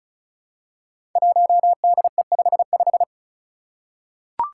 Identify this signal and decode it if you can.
Morse code, heard as a keyed tone, about 36 wpm, 700 Hz. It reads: 1DE55
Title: Five Challenging Letter-Number Combinations 35wpm